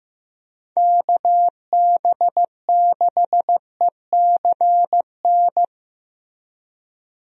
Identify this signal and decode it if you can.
Morse code, heard as a keyed tone, about 16 wpm, 700 Hz. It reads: KB6ECN
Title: USA Call Signs - 6 Characters - 15wpm